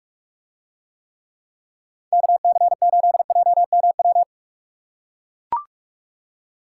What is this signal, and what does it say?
Morse code, heard as a keyed tone, about 42 wpm, 700 Hz. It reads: KC8JMW